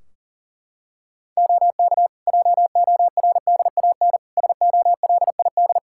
Morse code, heard as a keyed tone, about 40 wpm, 700 Hz. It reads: OK JORDAN SOLID
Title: Practice QSO - Exchange 3 - More Info - 40wpm